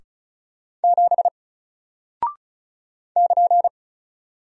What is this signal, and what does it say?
Morse code, heard as a keyed tone, about 36 wpm, 700 Hz. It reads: 7 <KN>
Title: Single Character (Letter:Number:Punctuation:Prosign) Alt 35wpm